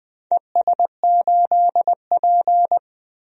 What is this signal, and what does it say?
Morse code, heard as a keyed tone, about 20 wpm, 700 Hz. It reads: ES8P
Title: DXCC-100 Call Signs - 4 Characters - Repeated 2x - 20wpm